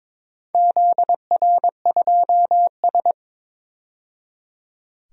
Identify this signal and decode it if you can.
Morse code, heard as a keyed tone, about 22 wpm, 700 Hz. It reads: ZR2S
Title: DXCC-100 Call Signs - 4 Characters - 22wpm